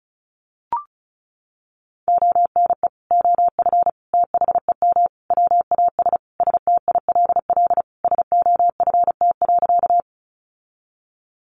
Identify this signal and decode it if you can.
Morse code, heard as a keyed tone, about 34 wpm, 700 Hz. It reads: ONE OF THEM WAS STILL SOFT.
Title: Sentences from Top 1500 Words 35wpm